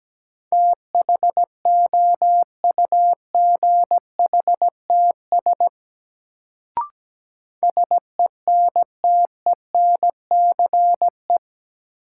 Morse code, heard as a keyed tone, about 17 wpm, 700 Hz. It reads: THOUGHTS SENTENCE
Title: Sentences from Top 1000 Words - Review Part 1 - New Word Drill - 17wpm